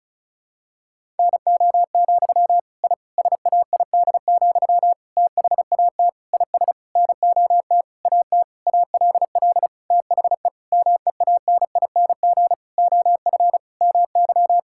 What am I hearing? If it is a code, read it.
Morse code, heard as a keyed tone, about 36 wpm, 700 Hz. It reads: NO, I SAID, THAT IS NOT AT ALL THE MEANING OF MY
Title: Sentences from Top 1000 Words - Review Part 6 - 35wpm